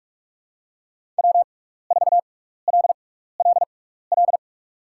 Morse code, heard as a keyed tone, about 42 wpm, spaced as for 15 wpm, 690 Hz. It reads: W4LLL
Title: USA Call Signs - 5 Characters - 6x Character Spacing - 45wpm